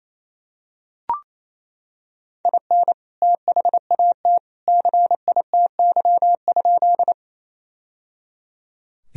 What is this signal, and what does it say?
Morse code, heard as a keyed tone, about 28 wpm, 700 Hz. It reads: IN THAT CITY?
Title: Sentences from Top 1000 Words - Review Part 1 - 28wpm